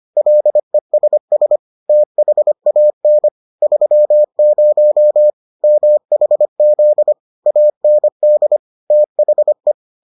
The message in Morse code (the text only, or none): LESS THAN 30 MHZ AND THE